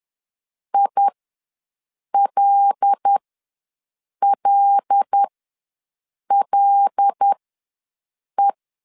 Morse code, heard as a keyed tone, about 10 wpm, spaced as for 6 wpm, 800 Hz. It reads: ILLLE
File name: LEZIONI / 5 LBVC / 5 L ws 10wpm 3636